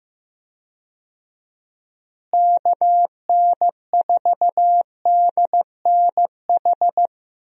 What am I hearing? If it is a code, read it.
Morse code, heard as a keyed tone, about 15 wpm, 700 Hz. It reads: KN4DNH
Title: USA Call Signs - 6 Characters - Repeated 3x - 15wpm